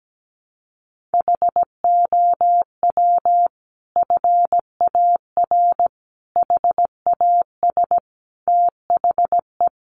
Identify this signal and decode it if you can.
Morse code, heard as a keyed tone, about 17 wpm, 700 Hz. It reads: HOW FAR HAS THE
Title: Sentences from Top 1200 Words 17wpm